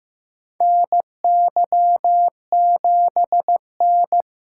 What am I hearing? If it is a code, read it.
Morse code, heard as a keyed tone, about 15 wpm, 700 Hz. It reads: NY7N
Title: K1USN SST QSO Practice - 15wpm